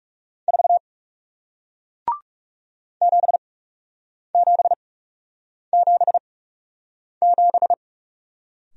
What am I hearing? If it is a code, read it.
Morse code, heard as a keyed tone, about 41 wpm, spaced as for 8 wpm, 700 Hz. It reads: 4 7777